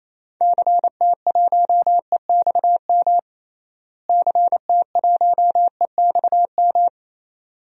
Morse code, heard as a keyed tone, about 27 wpm, 700 Hz. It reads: CT1EXM CT1EXM
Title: DXCC-100 Call Signs - 6 Characters - Repeated 2x - 28wpm